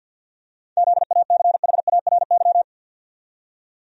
Morse code, heard as a keyed tone, about 52 wpm, 700 Hz. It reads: ZACHARY
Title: USA Common Names - 50wpm